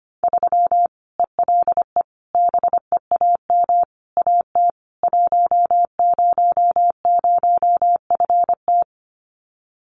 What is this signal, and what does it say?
Morse code, heard as a keyed tone, about 25 wpm, 700 Hz. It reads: 3 ELE BEAM AT 100FT
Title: Common QSO Phrases 25wpm